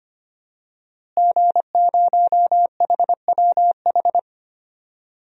Morse code, heard as a keyed tone, about 25 wpm, 700 Hz. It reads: G0HWH
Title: DXCC-100 Call Signs - 5 Characters - Repeated 3x - 25wpm